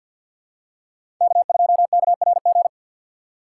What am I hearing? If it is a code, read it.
Morse code, heard as a keyed tone, about 51 wpm, 700 Hz. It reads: K2KRG
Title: USA Call Signs - 5 Characters - 4x Character Spacing - 50wpm